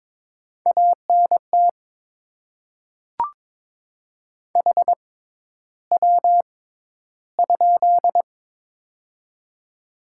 Morse code, heard as a keyed tone, about 22 wpm, spaced as for 11 wpm, 700 Hz. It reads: ANT HW?
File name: Top 20 QSO Elements - 6x Character Spacing - 22wpm